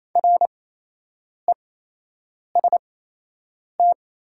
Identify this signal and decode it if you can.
Morse code, heard as a keyed tone, about 28 wpm, spaced as for 8 wpm, 700 Hz. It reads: REST
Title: 8x Character Spacing - Top 500 Words - 28wpm